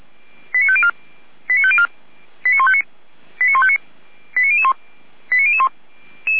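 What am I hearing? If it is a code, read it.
Five-tone selective call (ZVEI), heard as quick runs of five tones, tones of about 80 ms. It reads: 89504 89504 89159 89159 89001 89001
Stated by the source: ZVEI five-tone selective call received with an SDR, fire brigade feuerwehr-orlamuende.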